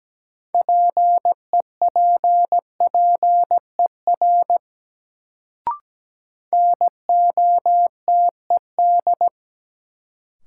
Morse code, heard as a keyed tone, about 17 wpm, 700 Hz. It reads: PEPPER NOTED
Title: Sets of 5 Words - New Word Drill - 17wpm